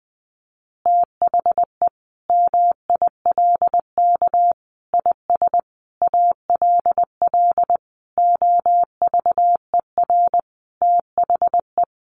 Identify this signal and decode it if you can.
Morse code, heard as a keyed tone, about 20 wpm, 700 Hz. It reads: THE MILK IS ALL OVER THE
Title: Additional Easy Phrases 20wpm